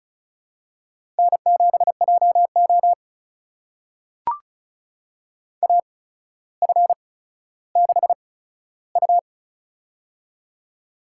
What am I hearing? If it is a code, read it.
Morse code, heard as a keyed tone, about 36 wpm, spaced as for 15 wpm, 700 Hz. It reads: N7JO AF6U
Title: USA Call Signs - 4 Characters - 8x Character Spacing - 35wpm